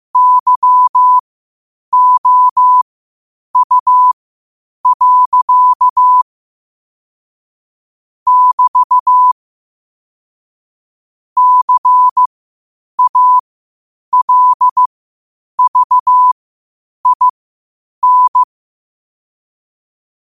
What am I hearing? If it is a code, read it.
Morse code, heard as a keyed tone, about 15 wpm, spaced as for 8 wpm, 1000 Hz. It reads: YOU. <BT> CALVIN
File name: qotdfiles / 8wpm